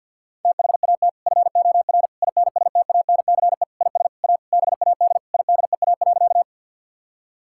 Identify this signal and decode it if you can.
Morse code, heard as a keyed tone, about 50 wpm, 700 Hz. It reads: THAT FOR INSTANCE IS A BAD IDEA.
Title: Sentences from Top 1000 Words - Review Part 5 - 50wpm